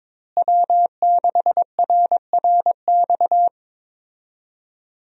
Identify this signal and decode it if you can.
Morse code, heard as a keyed tone, about 21 wpm, 700 Hz. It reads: W6RRX
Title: USA Call Signs - 5 Characters - Repeated 2x - 22wpm